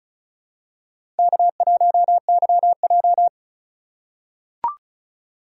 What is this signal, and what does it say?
Morse code, heard as a keyed tone, about 36 wpm, 700 Hz. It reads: K1YJ
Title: USA Call Signs - 4 Characters - 2x Character Spacing - 35wpm